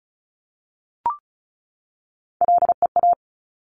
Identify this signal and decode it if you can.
Morse code, heard as a keyed tone, about 36 wpm, 700 Hz. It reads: LEA
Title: Top Three Letter Combinations in Words 35wpm